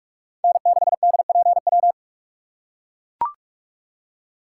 Morse code, heard as a keyed tone, about 44 wpm, 700 Hz. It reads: N6DPW